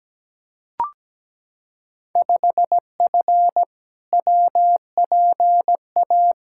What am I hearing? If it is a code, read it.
Morse code, heard as a keyed tone, about 17 wpm, 700 Hz. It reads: 5F WPA
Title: Field Day Exchanges Encore - 17wpm